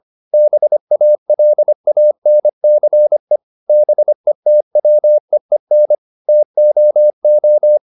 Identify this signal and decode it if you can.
Morse code, heard as a keyed tone, about 25 wpm, 600 Hz. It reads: BALANCE BETWEEN TOO